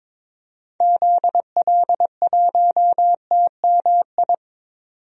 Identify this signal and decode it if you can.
Morse code, heard as a keyed tone, about 22 wpm, 700 Hz. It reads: ZL1TMI
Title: DXCC-100 Call Signs - 6 Characters - 22wpm